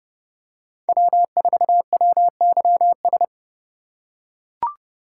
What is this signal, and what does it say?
Morse code, heard as a keyed tone, about 29 wpm, 700 Hz. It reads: W4WYS